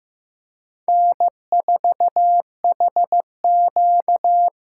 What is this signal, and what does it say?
Morse code, heard as a keyed tone, about 15 wpm, 700 Hz. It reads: N4HQ
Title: USA Call Signs - 5 Characters - 6x Character Spacing - 15wpm